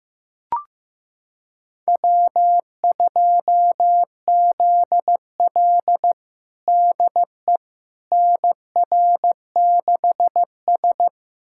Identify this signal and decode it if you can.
Morse code, heard as a keyed tone, about 15 wpm, 700 Hz. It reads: W2ZL DE NR6S
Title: USA Callsign DE Callsign - 4 Characters - 15wpm